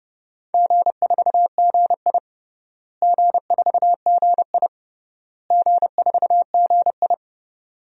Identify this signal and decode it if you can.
Morse code, heard as a keyed tone, about 30 wpm, 700 Hz. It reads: G4GI G4GI G4GI
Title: DXCC-100 Call Signs - 4 Characters - Repeated 3x - 30wpm